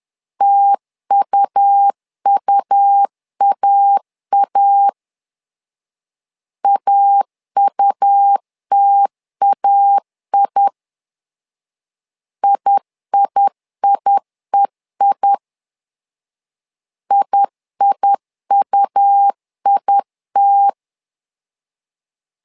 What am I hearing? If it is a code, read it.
Morse code, heard as a keyed tone, about 10 wpm, 800 Hz. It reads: TUUAA AUTAI IIIEI IIUIT
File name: LEZIONI / 2 UHDN / 2 U 10wpm 1622